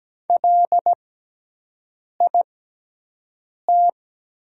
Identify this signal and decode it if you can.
Morse code, heard as a keyed tone, about 17 wpm, spaced as for 6 wpm, 700 Hz. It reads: LIT
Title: 6x Character Spacing - Top 200 Words - 17wpm